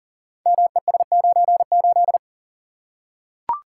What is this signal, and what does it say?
Morse code, heard as a keyed tone, about 38 wpm, 690 Hz. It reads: MES98
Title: Code Groups - 5 Letter:Numbers - 40wpm